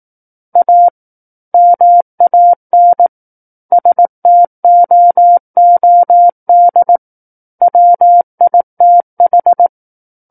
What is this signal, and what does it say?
Morse code, heard as a keyed tone, about 18 wpm, 700 Hz. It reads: A MAN STOOD WITH
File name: 18 WPM